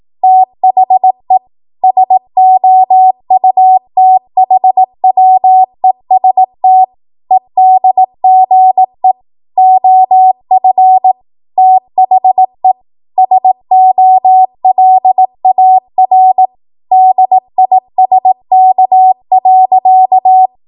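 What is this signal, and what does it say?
Morse code, heard as a keyed tone, about 18 wpm, 750 Hz. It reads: THE SOUTHWEST EDGE OF THE SOLAR DISK.
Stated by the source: Code practice at 18 WPM, a transition run bridging 15 to 20 WPM.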